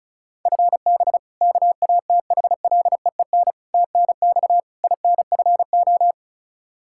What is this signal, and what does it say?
Morse code, heard as a keyed tone, about 34 wpm, 700 Hz. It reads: FB KATHLEEN TNX INFO
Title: Practice QSO - Exchange 3 - More Info - 35wpm